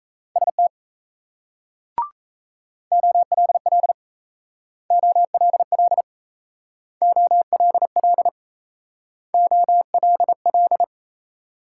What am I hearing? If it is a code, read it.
Morse code, heard as a keyed tone, about 35 wpm, 700 Hz. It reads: ST OLL OLL OLL OLL
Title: Speed-Racing Multi-letter Phonemes - 28wpm